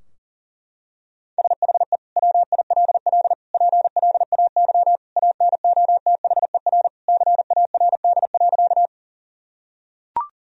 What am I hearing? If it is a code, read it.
Morse code, heard as a keyed tone, about 37 wpm, 700 Hz. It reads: SHE WILL PLAY ANOTHER CARD.
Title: Sentences from Top 1000 Words - Review Part 4 - 40wpm